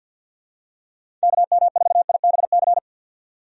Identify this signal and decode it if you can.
Morse code, heard as a keyed tone, about 51 wpm, 700 Hz. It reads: KM4IBC
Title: USA Call Signs - 6 Characters - 50wpm